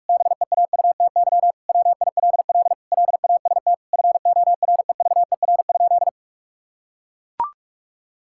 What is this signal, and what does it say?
Morse code, heard as a keyed tone, about 41 wpm, 700 Hz. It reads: BEAUTY WILL LAST FOREVER?
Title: Sentences from Top 1600 Words 45wpm